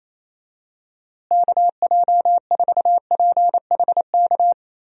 Morse code, heard as a keyed tone, about 28 wpm, 700 Hz. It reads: KJ4PHK
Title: USA Call Signs - 6 Characters - Repeated 3x - 28wpm